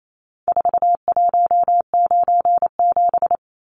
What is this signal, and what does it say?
Morse code, heard as a keyed tone, about 28 wpm, 700 Hz. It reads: 4197
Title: Numbers - 4-Digit 28wpm